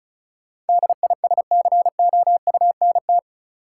Morse code, heard as a keyed tone, about 34 wpm, 700 Hz. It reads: DISCOUNT
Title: Sentences from Top 1200 Words - New Word Drill - 35wpm